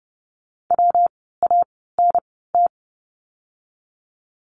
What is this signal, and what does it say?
Morse code, heard as a keyed tone, about 30 wpm, spaced as for 17 wpm, 700 Hz. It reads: WANT
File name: course / 30:10wpm / 230 Lesson 37f - Letter x - Words 30wpm